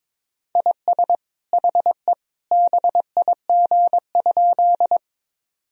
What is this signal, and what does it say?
Morse code, heard as a keyed tone, about 22 wpm, 700 Hz. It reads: IS HE BIG?
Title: Sentences from Top 1000 Words - Review Part 1 - 22wpm